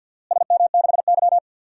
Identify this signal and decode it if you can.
Morse code, heard as a keyed tone, about 47 wpm, 700 Hz. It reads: SM6Y
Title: DXCC-100 Call Signs - 4 Characters - Repeated 3x - 50wpm